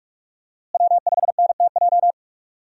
Morse code, heard as a keyed tone, about 46 wpm, 700 Hz. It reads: W5NTJ
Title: USA Call Signs - 5 Characters - Repeated 3x - 45wpm